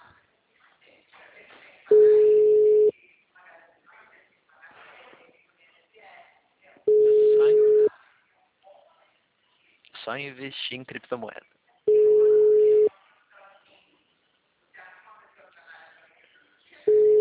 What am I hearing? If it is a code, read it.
Ringback tone (Europe, 425 Hz), with speech.